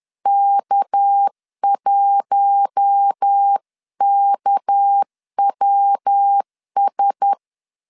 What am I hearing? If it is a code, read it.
Morse code, heard as a keyed tone, about 10 wpm, 800 Hz. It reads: K1KWS
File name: LEZIONI / 6 PKX0 / 6 PK 10wpm 2436